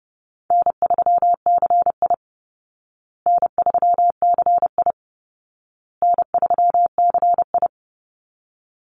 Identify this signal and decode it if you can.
Morse code, heard as a keyed tone, about 30 wpm, 700 Hz. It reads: N3CI N3CI N3CI
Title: Call Signs Repeated 2x - 30wpm